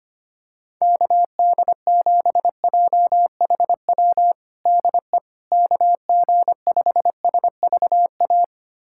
Morse code, heard as a keyed tone, about 25 wpm, 700 Hz. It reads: KD7JHW DE KG5SVA
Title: USA Callsign DE Callsign - 6 Characters - 25wpm